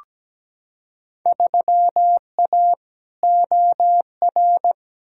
Morse code, heard as a keyed tone, about 17 wpm, 700 Hz. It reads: 3A OR
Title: Field Day Exchanges Encore - 17wpm